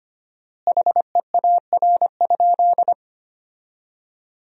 Morse code, heard as a keyed tone, about 25 wpm, 700 Hz. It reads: HEAR?
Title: 4x Word-Spacing - Sentences from Top 300 Words - 25wpm